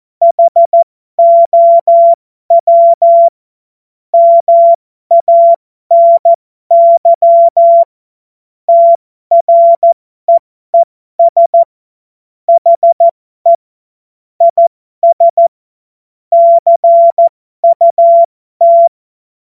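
Morse code, heard as a keyed tone, about 14 wpm, spaced as for 12 wpm, 680 Hz. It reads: HOW MANY TREES HE IS CUT
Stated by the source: Amateur radio CW practice text made of jokes.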